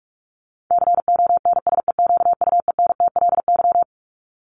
Morse code, heard as a keyed tone, about 44 wpm, 700 Hz. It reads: CONSEQUENTLY
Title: LY Suffix - Top 100 Words with Base Words 45wpm